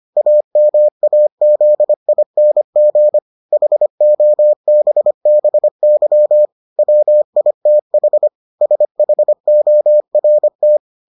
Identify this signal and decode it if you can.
Morse code, heard as a keyed tone, about 25 wpm, 600 Hz. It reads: AMAZING HOBBY WITH SHORT